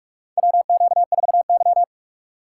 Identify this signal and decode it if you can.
Morse code, heard as a keyed tone, about 44 wpm, 700 Hz. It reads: WQ4Y
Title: USA Call Signs - 3x Character Spacing - 45wpm